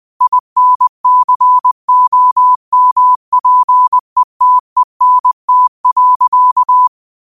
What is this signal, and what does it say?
Morse code, heard as a keyed tone, about 20 wpm, 1000 Hz. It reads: INCOMPETENT.
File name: qotdfiles / 20wpm